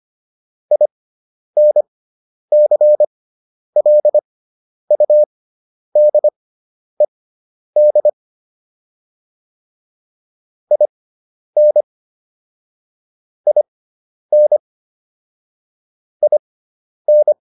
Morse code, heard as a keyed tone, about 25 wpm, spaced as for 10 wpm, 600 Hz. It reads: INCLUDED IN IN IN